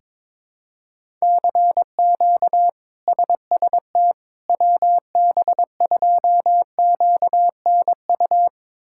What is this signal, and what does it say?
Morse code, heard as a keyed tone, about 22 wpm, 700 Hz. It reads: CQ SST WB2QNU
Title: K1USN SST QSO Practice - 22wpm